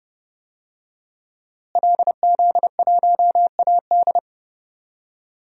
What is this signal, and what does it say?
Morse code, heard as a keyed tone, about 30 wpm, 700 Hz. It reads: LZ1AD
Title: DXCC-100 Call Signs - 5 Characters - Repeated 2x - 30wpm